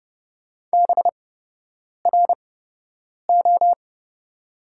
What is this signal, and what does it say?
Morse code, heard as a keyed tone, about 30 wpm, spaced as for 8 wpm, 700 Hz. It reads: BRO
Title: 8x Character Spacing - Top 500 Words - 30wpm